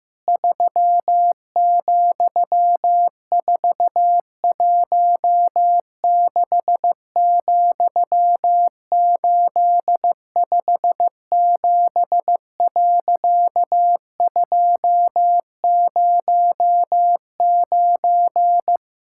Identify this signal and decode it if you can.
Morse code, heard as a keyed tone, about 16 wpm, 700 Hz. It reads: 3,416,857.209
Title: Pangram Challenge v2 - 15wpm